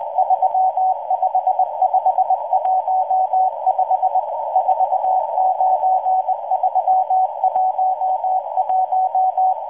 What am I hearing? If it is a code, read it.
Morse code, heard as a keyed tone, about 20 wpm, 750 Hz. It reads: 35515 49311